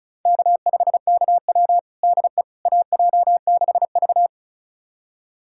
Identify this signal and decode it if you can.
Morse code, heard as a keyed tone, about 34 wpm, 690 Hz. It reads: K5KW DE AJ6V
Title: USA Callsign DE Callsign - 4 Characters - 35wpm